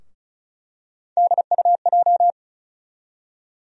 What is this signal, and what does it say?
Morse code, heard as a keyed tone, about 34 wpm, 700 Hz. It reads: DUJ
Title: USA Call Sign Postfixes - 3 Letters 35wpm